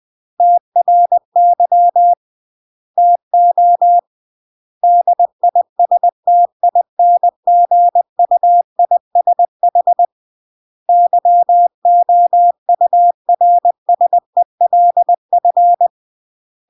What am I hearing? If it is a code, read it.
Morse code, heard as a keyed tone, about 20 wpm, 700 Hz. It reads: TRY TO DISTINGUISH YOURSELF
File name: CW / jodorowsky / 82 commandments 20-20